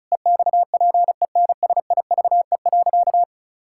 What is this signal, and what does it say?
Morse code, heard as a keyed tone, about 36 wpm, 700 Hz. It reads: EXPENSIVE.
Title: Pangram Challenge v2 - 35wpm